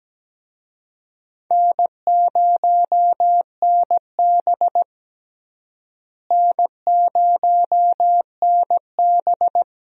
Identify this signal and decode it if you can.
Morse code, heard as a keyed tone, about 17 wpm, 700 Hz. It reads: N0NB N0NB
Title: USA Call Signs - 4 Characters - Repeated 2x - 17wpm